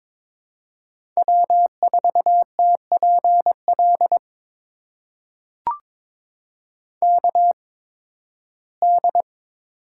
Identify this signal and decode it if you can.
Morse code, heard as a keyed tone, about 22 wpm, spaced as for 15 wpm, 700 Hz. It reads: W4TPL KD
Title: USA Call Signs - 8x Character Spacing - 22wpm